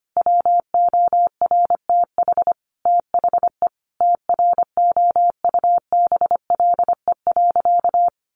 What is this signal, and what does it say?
Morse code, heard as a keyed tone, about 25 wpm, 700 Hz. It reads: WORTH THE TROUBLE.
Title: Sentences from Top 1500 Words 25wpm